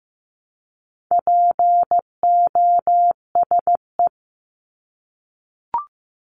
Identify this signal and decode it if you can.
Morse code, heard as a keyed tone, about 15 wpm, 700 Hz. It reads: POSE
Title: Top 500 Words Repeated 15wpm